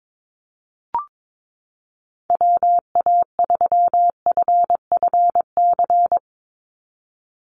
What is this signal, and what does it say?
Morse code, heard as a keyed tone, about 22 wpm, 700 Hz. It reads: WA3FFC
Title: USA Call Signs 22wpm